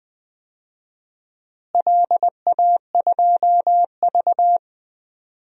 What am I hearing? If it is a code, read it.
Morse code, heard as a keyed tone, about 20 wpm, 700 Hz. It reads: LA2V
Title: DXCC-100 Call Signs - 4 Characters - Repeated 3x - 20wpm